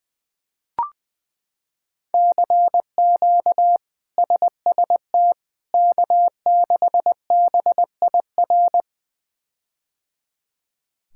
Morse code, heard as a keyed tone, about 20 wpm, 700 Hz. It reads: CQ SST K6BIR
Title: K1USN SST QSO Practice - 20wpm